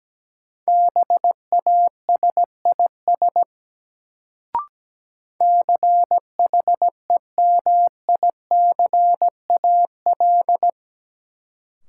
Morse code, heard as a keyed tone, about 17 wpm, 700 Hz. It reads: BASIS CHEMICAL
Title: Sentences from Top 1200 Words - New Word Drill Alt - 17wpm